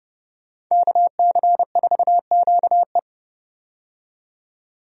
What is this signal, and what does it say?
Morse code, heard as a keyed tone, about 30 wpm, 700 Hz. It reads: KC4QE
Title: USA Call Signs - 5 Characters - 30wpm